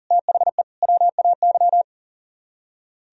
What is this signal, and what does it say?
Morse code, heard as a keyed tone, about 50 wpm, 700 Hz. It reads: THE WAY
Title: Speed-Racing - Sets of 3 Words 40wpm